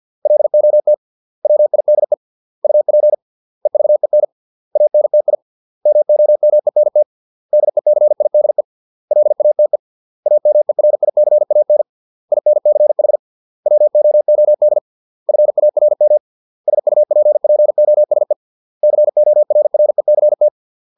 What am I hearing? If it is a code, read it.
Morse code, heard as a keyed tone, about 50 wpm, 600 Hz. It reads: LOT WIDE UP EVEN ANTI MOMENT DECIDE LATE AMERICAN INCH WOOD FARM SUPPOSE CORRECT